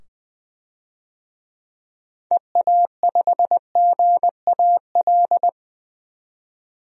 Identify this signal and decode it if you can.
Morse code, heard as a keyed tone, about 20 wpm, 700 Hz. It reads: EA5GAL
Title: DXCC-100 Call Signs - 6 Characters - 20wpm